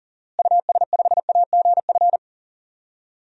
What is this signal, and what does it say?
Morse code, heard as a keyed tone, about 40 wpm, 700 Hz. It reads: US5AGF
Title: DXCC-100 Call Signs - 6 Characters - Repeated 2x - 40wpm